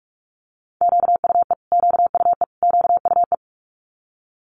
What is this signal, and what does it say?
Morse code, heard as a keyed tone, about 46 wpm, 700 Hz. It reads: QUE QUE QUE
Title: Multi-letter Phonemes Repeated 3x 45wpm